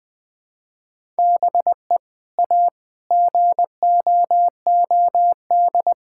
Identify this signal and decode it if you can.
Morse code, heard as a keyed tone, about 20 wpm, 700 Hz. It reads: BE A GOOD
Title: Sentences with 2-Word Phrases - Spoken Beginning - 20wpm